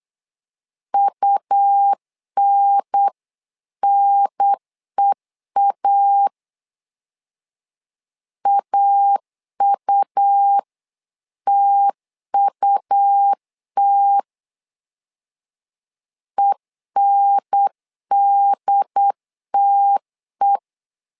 Morse code, heard as a keyed tone, about 8 wpm, 800 Hz. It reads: UNNEA AUTUT ENDTE
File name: LEZIONI / 3 MWRO / 3 M 8wpm 2300